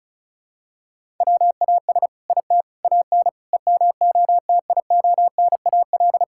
Morse code, heard as a keyed tone, about 36 wpm, 700 Hz. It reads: WAS IT AN EMOTIONAL
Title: Sentences from Top 1000 Words - Review Part 5 - 35wpm